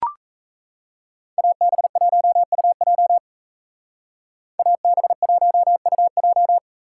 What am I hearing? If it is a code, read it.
Morse code, heard as a keyed tone, about 33 wpm, 700 Hz. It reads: AB1UJ AB1UJ
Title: Speed Racing - USA Call Signs 28wpm